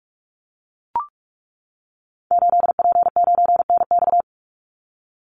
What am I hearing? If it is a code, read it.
Morse code, heard as a keyed tone, about 44 wpm, 700 Hz. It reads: 8P9NX
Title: Call Signs - 45wpm